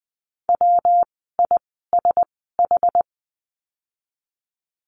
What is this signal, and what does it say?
Morse code, heard as a keyed tone, about 20 wpm, spaced as for 14 wpm, 700 Hz. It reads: WISH
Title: 115 Lesson 19f - Letter w - Words 20wpm